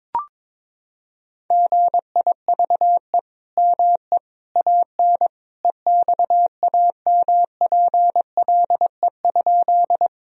Morse code, heard as a keyed tone, about 22 wpm, 700 Hz. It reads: GIVE ME AN EXAMPLE?
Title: Sentences from Top 1000 Words - Review Part 1 - 22wpm